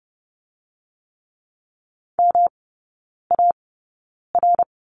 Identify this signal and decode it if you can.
Morse code, heard as a keyed tone, about 30 wpm, spaced as for 9 wpm, 700 Hz. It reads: MAR
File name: Lesson 18e - Letter m - Three Characters 30wpm